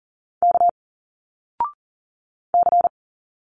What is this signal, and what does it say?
Morse code, heard as a keyed tone, about 39 wpm, 700 Hz. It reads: K C
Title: Single Letters 40wpm